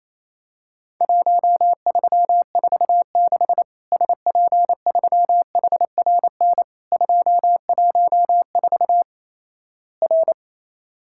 Morse code, heard as a keyed tone, about 28 wpm, 700 Hz. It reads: 1346 SP3HRN 214 R
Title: Worked All Europe DX Contest - QTC Exchange - 28wpm